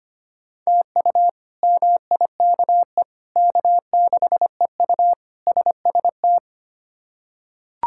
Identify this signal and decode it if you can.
Morse code, heard as a keyed tone, about 25 wpm, 700 Hz. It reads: TU MIKE K6EU SST
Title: K1USN SST QSO Practice - 25wpm